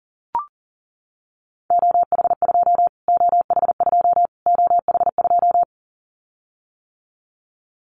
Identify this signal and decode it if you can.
Morse code, heard as a keyed tone, about 40 wpm, 700 Hz. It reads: OH2 OH2 OH2
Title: Callsign Prefixes for Top 25 DXCC - Repeated 3x - 40wpm